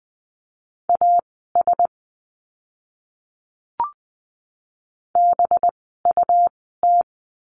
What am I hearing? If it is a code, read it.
Morse code, heard as a keyed tone, about 20 wpm, spaced as for 14 wpm, 700 Hz. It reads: AS BUT
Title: 185 Lesson 30f - Letter b - Words 20wpm